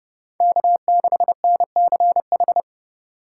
Morse code, heard as a keyed tone, about 28 wpm, 700 Hz. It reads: K6NCH